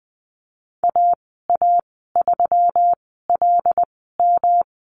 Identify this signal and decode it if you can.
Morse code, heard as a keyed tone, about 20 wpm, spaced as for 15 wpm, 700 Hz. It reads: AA3LM
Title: 186 Lesson 30g - Letter b - Call Signs 20wpm